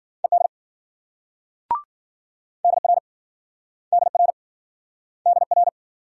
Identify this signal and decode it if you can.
Morse code, heard as a keyed tone, about 52 wpm, 700 Hz. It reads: ED DR DR DR